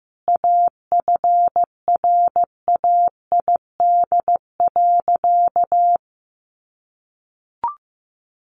Morse code, heard as a keyed tone, about 15 wpm, 700 Hz. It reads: AFRAID.
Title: Sentences from Top 1500 Words 15wpm